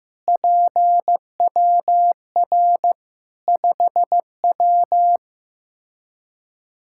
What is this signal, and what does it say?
Morse code, heard as a keyed tone, about 15 wpm, 700 Hz. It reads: PWR 5W
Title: Practice QSO - Exchange 2 - Info - 15wpm